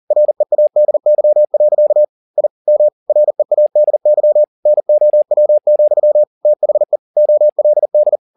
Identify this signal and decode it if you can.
Morse code, heard as a keyed tone, about 40 wpm, 600 Hz. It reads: READY. I M READY NOW, THE OLD